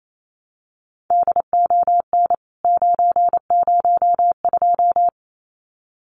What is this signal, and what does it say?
Morse code, heard as a keyed tone, about 28 wpm, 700 Hz. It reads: DON 902
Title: CWT Exchange 28wpm